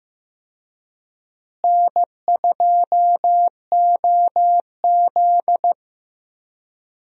Morse code, heard as a keyed tone, about 15 wpm, 700 Hz. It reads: N2OZ
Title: USA Call Signs - 4 Characters - Repeated 2x - 15wpm